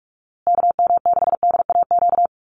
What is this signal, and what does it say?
Morse code, heard as a keyed tone, about 44 wpm, 700 Hz. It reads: KM6DAQ
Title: USA Call Signs Repeated 3x 45wpm